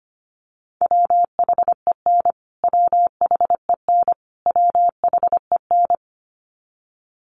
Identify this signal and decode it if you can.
Morse code, heard as a keyed tone, about 25 wpm, 700 Hz. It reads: WHEN WHEN WHEN
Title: Top 500 Words Repeated 25wpm